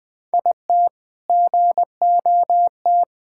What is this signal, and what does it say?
Morse code, heard as a keyed tone, about 21 wpm, 700 Hz. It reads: IT GOT
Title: All-Age Appropriate Jokes - 20wpm